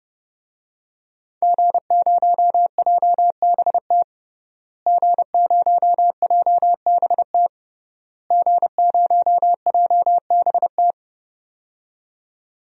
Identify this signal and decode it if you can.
Morse code, heard as a keyed tone, about 30 wpm, 700 Hz. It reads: G0JBT G0JBT G0JBT
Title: DXCC-100 Call Signs - 5 Characters - Repeated 3x - 30wpm